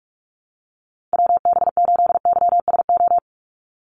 Morse code, heard as a keyed tone, about 46 wpm, 690 Hz. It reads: WB8YSO
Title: USA Call Signs 45wpm